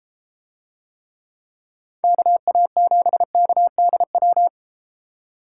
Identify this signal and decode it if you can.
Morse code, heard as a keyed tone, about 31 wpm, 700 Hz. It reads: KA7KDW